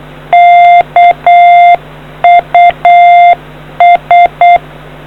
Morse code, heard as a keyed tone, about 8 wpm, 700 Hz. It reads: KUS